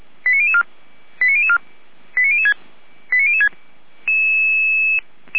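Five-tone selective call (ZVEI), heard as quick runs of five tones, tones of about 70 ms. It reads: 89004 89004 89006 89006
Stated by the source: ZVEI five-tone selective call received on SDR.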